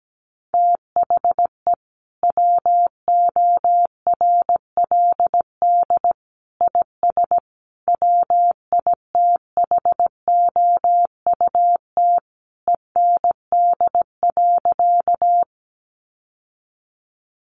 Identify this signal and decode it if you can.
Morse code, heard as a keyed tone, about 17 wpm, 700 Hz. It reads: THE WORLD IS WITHOUT END.
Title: Sentences from Top 600 Words 17wpm